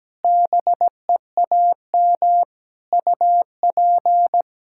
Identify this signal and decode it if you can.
Morse code, heard as a keyed tone, about 17 wpm, 700 Hz. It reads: BEAM UP
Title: Practice QSO - Exchange 2 - Info - 17wpm